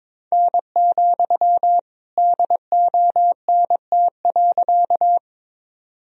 Morse code, heard as a keyed tone, about 21 wpm, 700 Hz. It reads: N, DONT.
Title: Sentences from Top 300 Words and Abbreviations 22wpm